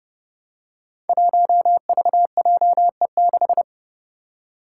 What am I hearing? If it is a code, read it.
Morse code, heard as a keyed tone, about 30 wpm, 700 Hz. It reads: 1VJE6
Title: Five Challenging Letter-Number Combinations 30wpm